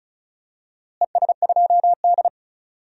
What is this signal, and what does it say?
Morse code, heard as a keyed tone, about 35 wpm, 700 Hz. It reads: ES2D
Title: DXCC-100 Call Signs - 4 Characters - Repeated 3x - 35wpm